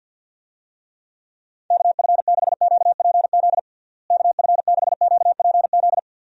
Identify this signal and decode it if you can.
Morse code, heard as a keyed tone, about 51 wpm, 700 Hz. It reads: KF6QPZ KF6QPZ
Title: USA Call Signs - 6 Characters - Repeated 3x - 50wpm